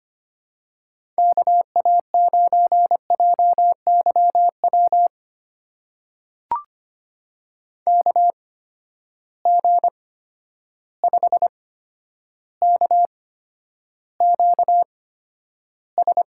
Morse code, heard as a keyed tone, about 25 wpm, spaced as for 11 wpm, 700 Hz. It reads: KA9JYW KG5KQS
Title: USA Call Signs - 6 Characters - 8x Character Spacing - 25wpm